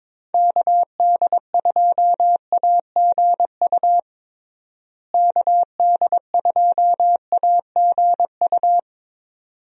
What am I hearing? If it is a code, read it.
Morse code, heard as a keyed tone, about 22 wpm, 700 Hz. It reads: KD2AGU KD2AGU
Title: USA Call Signs - 6 Characters - Repeated 2x - 22wpm